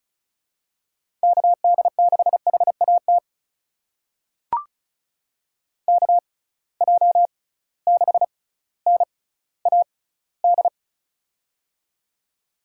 Morse code, heard as a keyed tone, about 34 wpm, spaced as for 18 wpm, 700 Hz. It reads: KD6HAT KJ6NAD
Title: USA Call Signs - 6 Characters - 6x Character Spacing - 35wpm